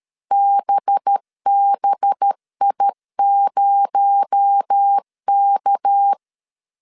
Morse code, heard as a keyed tone, about 13 wpm, 800 Hz. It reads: BBI0K
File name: CW / DET / 1 6 1500C S B 12wpm 3431